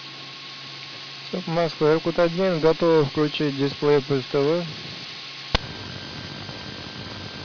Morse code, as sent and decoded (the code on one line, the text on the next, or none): none